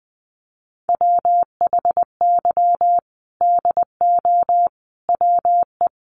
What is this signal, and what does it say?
Morse code, heard as a keyed tone, about 20 wpm, 700 Hz. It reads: WHY DO WE
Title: Sentences from Top 400 Words 20wpm